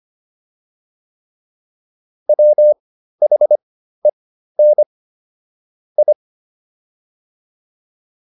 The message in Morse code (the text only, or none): WHEN I